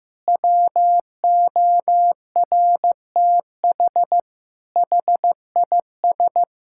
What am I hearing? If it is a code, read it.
Morse code, heard as a keyed tone, about 16 wpm, 700 Hz. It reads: WORTH HIS
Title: Sentences from Top 1000 Words - Review Part 5 - 15wpm